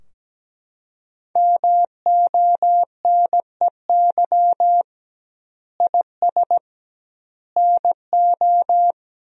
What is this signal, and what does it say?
Morse code, heard as a keyed tone, about 17 wpm, 700 Hz. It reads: MONEY IS NO
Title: 2x Word-Spacing - Sentences from Top 500 Words - 17wpm